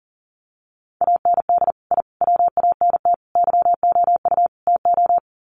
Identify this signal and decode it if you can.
Morse code, heard as a keyed tone, about 40 wpm, 700 Hz. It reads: AND I WANT YOU TO
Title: Sets of 5 Words Encore 40wpm